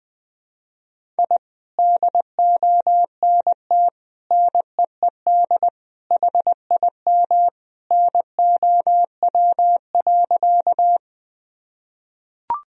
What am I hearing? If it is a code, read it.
Morse code, heard as a keyed tone, about 20 wpm, 700 Hz. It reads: I DONT NEED HIM NOW.
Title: Sentences from Top 1000 Words - Review Part 1 - 20wpm